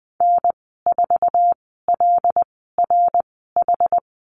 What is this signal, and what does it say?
Morse code, heard as a keyed tone, about 20 wpm, spaced as for 14 wpm, 700 Hz. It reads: N4LRH
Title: 088 Lesson 15g - Number 5 - Call Signs 20wpm